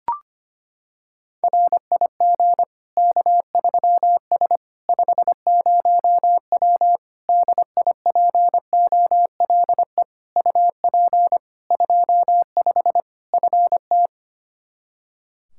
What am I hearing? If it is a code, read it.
Morse code, heard as a keyed tone, about 25 wpm, 700 Hz. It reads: RIG K3S 50W DIPOLE UP 25 FT
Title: Practice QSO - Exchange 2 - Info - 25wpm